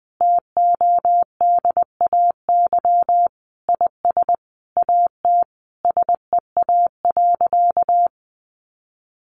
Morse code, heard as a keyed tone, about 20 wpm, 700 Hz. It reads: TODAY IS AT SEA.
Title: Sentences from Top 1400 Words 20wpm